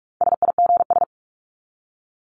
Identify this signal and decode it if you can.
Morse code, heard as a keyed tone, about 44 wpm, 700 Hz. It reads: SIGS
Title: Abbreviations 45wpm